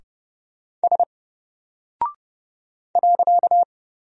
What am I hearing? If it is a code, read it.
Morse code, heard as a keyed tone, about 29 wpm, 700 Hz. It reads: S .
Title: Single Character (Letter:Number:Punctuation:Prosign) 30wpm